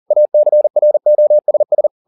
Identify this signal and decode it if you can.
Morse code, heard as a keyed tone, about 40 wpm, 600 Hz. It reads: ACROSS
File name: the old man and the sea 2 40wpm